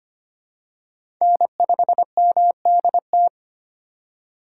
Morse code, heard as a keyed tone, about 25 wpm, 700 Hz. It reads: N5MDT
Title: USA Call Signs - 5 Characters - 8x Character Spacing - 25wpm